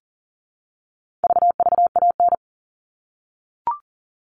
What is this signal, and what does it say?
Morse code, heard as a keyed tone, about 40 wpm, 700 Hz. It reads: VVAN
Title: All Cut Numbers - 4-Digit 40wpm